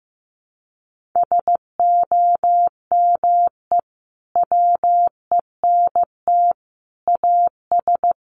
Morse code, heard as a keyed tone, about 15 wpm, 700 Hz. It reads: SOME WENT AS
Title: Sentences from Top 1100 Words 15wpm